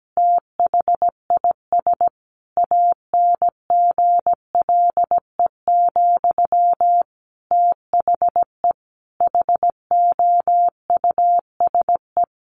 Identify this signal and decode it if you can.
Morse code, heard as a keyed tone, about 17 wpm, 700 Hz. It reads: THIS ANGLE, THE HOUSE
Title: Sentences from Top 1500 Words 17wpm